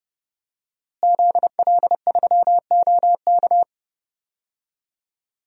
Morse code, heard as a keyed tone, about 32 wpm, 700 Hz. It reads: ZL3OK
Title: DXCC-100 Call Signs - 5 Characters - Repeated 2x - 30wpm